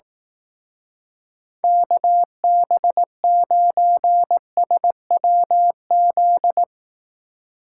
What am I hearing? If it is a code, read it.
Morse code, heard as a keyed tone, about 18 wpm, 700 Hz. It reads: KB9SWZ
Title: Speed Racing - USA Call Signs 15wpm